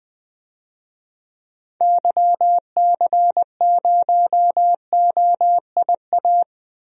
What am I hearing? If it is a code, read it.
Morse code, heard as a keyed tone, about 21 wpm, 700 Hz. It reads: YC0OIA
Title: DXCC-100 Call Signs - 6 Characters - Repeated 2x - 20wpm